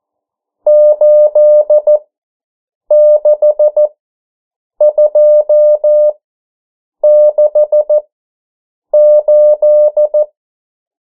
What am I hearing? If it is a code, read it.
Morse code, heard as a keyed tone, about 14 wpm, spaced as for 7 wpm, 600 Hz. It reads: 86268
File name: morse mp3 / 09 urok 8